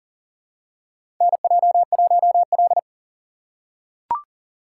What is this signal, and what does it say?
Morse code, heard as a keyed tone, about 40 wpm, 700 Hz. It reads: NJ1L
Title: USA Call Signs - 4 Characters - 2x Character Spacing - 40wpm